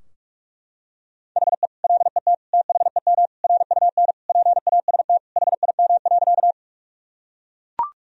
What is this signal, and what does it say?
Morse code, heard as a keyed tone, about 41 wpm, 700 Hz. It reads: HE LET THEM RUN PAST HIM.
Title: Sentences from Top 1000 Words - Review Part 3 - 45wpm